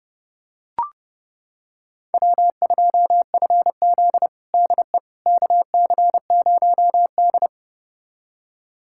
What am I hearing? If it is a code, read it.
Morse code, heard as a keyed tone, about 30 wpm, 700 Hz. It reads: W2FZ DE KC0D
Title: USA Callsign DE Callsign 30wpm